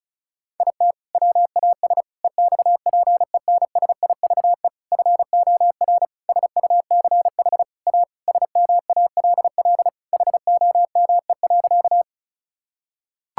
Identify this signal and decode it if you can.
Morse code, heard as a keyed tone, about 34 wpm, 700 Hz. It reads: IT WAS EXPENSIVE FOR SUCH A SMALL HOME.
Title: Sentences from Top 1000 Words - Review Part 6 - 35wpm